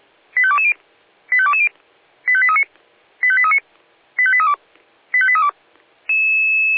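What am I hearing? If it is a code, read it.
Five-tone selective call (ZVEI), heard as quick runs of five tones, tones of about 70 ms. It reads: 85220 85220 85839 85839 85832 85832
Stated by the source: Five-tone ZVEI selective call received by SDR.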